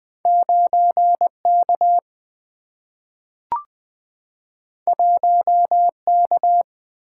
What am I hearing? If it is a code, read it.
Morse code, heard as a keyed tone, about 20 wpm, 700 Hz. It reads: 9K 1K
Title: Two Letter-Number Combinations - 20wpm